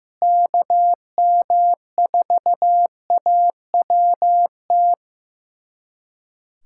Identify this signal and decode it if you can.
Morse code, heard as a keyed tone, about 15 wpm, 700 Hz. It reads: KM4AWT
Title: USA Callsign DE Callsign - 6 Characters - 15wpm